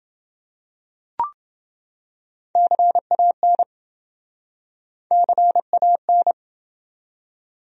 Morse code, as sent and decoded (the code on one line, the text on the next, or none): -.-. .- -. / -.-. .- -.
CAN CAN